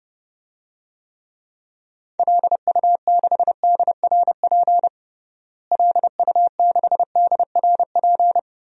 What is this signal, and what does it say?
Morse code, heard as a keyed tone, about 31 wpm, 700 Hz. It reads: LU6DRP LU6DRP
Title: DXCC-100 Call Signs - 6 Characters - Repeated 3x - 30wpm